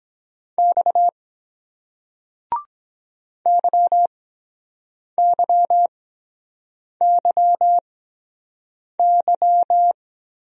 Morse code, heard as a keyed tone, about 23 wpm, spaced as for 7 wpm, 700 Hz. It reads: X YYYY